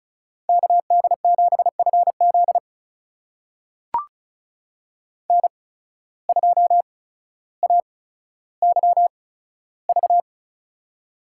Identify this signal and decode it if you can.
Morse code, heard as a keyed tone, about 36 wpm, spaced as for 15 wpm, 700 Hz. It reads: KD7FZ N2AYV
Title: USA Call Signs - 5 Characters - 8x Character Spacing - 35wpm